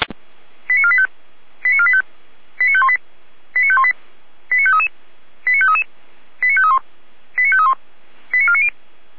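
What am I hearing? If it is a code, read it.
Five-tone selective call (ZVEI), heard as quick runs of five tones, tones of about 70 ms. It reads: 89485 89485 89518 89518 89533 89533 89531 89531 89490
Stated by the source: ZVEI five-tone selective call 89584 received with an SDR.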